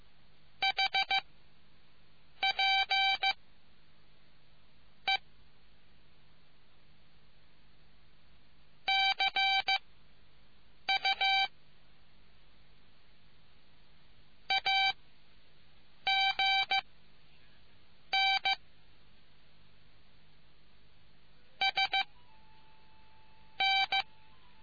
Morse code, heard as a keyed tone, about 15 wpm, spaced as for 6 wpm, 780 Hz. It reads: HP E CU AG N S N